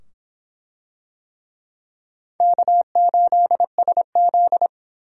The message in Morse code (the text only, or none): K8SZ